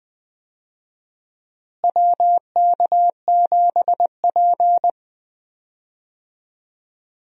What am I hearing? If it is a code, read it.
Morse code, heard as a keyed tone, about 20 wpm, 700 Hz. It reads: WK7P
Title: USA Call Signs - 4 Characters - Repeated 2x - 20wpm